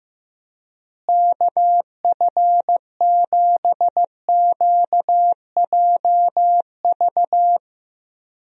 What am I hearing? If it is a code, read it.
Morse code, heard as a keyed tone, about 14 wpm, 700 Hz. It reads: KF7QJV
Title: USA Call Signs - 6 Characters - Repeated 3x - 15wpm